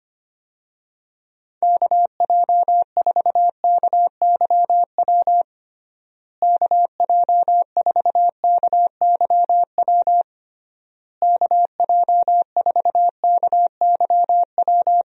Morse code, heard as a keyed tone, about 25 wpm, 700 Hz. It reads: KJ4KYW KJ4KYW KJ4KYW
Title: USA Call Signs - 6 Characters - Repeated 3x - 25wpm